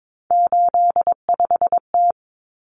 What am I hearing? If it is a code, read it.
Morse code, heard as a keyed tone, about 22 wpm, 700 Hz. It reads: 85T
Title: Common Cut Numbers - 3-Digit 22wpm